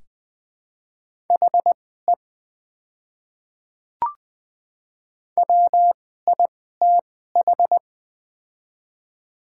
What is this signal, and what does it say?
Morse code, heard as a keyed tone, about 20 wpm, spaced as for 15 wpm, 700 Hz. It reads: HE WITH
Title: 2x Character Spacing - Top 100 Words - 20wpm